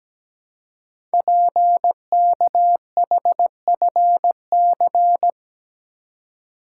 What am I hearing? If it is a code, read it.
Morse code, heard as a keyed tone, about 17 wpm, 700 Hz. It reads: PKHFC
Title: Code Groups - 5 Letters - 17wpm